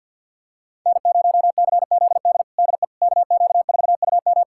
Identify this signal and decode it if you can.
Morse code, heard as a keyed tone, about 51 wpm, 700 Hz. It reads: N0CZD DE KQ4UM
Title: USA Callsign DE Callsign - 5 Characters - 50wpm